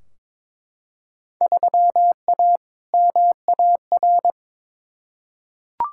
Morse code, heard as a keyed tone, about 22 wpm, 700 Hz. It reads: 3A MAR
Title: Field Day Exchanges Encore - 22wpm